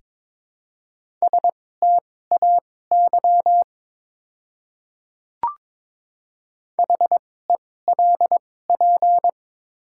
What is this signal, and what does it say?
Morse code, heard as a keyed tone, about 22 wpm, spaced as for 16 wpm, 700 Hz. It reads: STAY HELP
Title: 2x Character Spacing - Top 500 Words - 22wpm